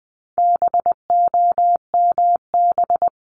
Morse code, heard as a keyed tone, about 20 wpm, 700 Hz. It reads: BOMB
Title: 260 Lesson 42e - Words 20-20wpm